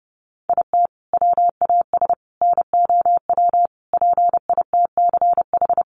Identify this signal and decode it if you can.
Morse code, heard as a keyed tone, about 28 wpm, 700 Hz. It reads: IT WAS NOW PITCH
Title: Sentences from Top 1300 Words 30wpm